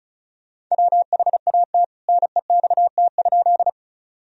Morse code, heard as a keyed tone, about 36 wpm, 700 Hz. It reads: WHAT NEXT?
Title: Sentences from Top 1000 Words - Review Part 1 - 35wpm